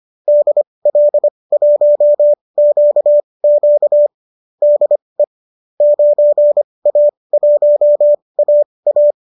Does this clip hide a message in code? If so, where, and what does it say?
Yes: Morse code from 0.3 to 9.1 s, heard as a keyed tone, about 25 wpm, spaced as for 20 wpm, 590 Hz. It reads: DL1QQ DE 9A1AA